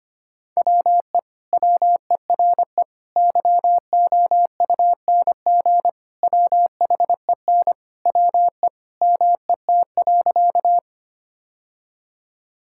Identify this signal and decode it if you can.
Morse code, heard as a keyed tone, about 25 wpm, 700 Hz. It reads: WE WERE YOUNG WHEN WE MET.
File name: practice / 25wpm / Sentences with 2-Word Phrases - 25wpm